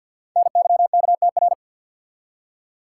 Morse code, heard as a keyed tone, about 50 wpm, 700 Hz. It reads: NYKTR
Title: Code Groups - 5 Letters - 50wpm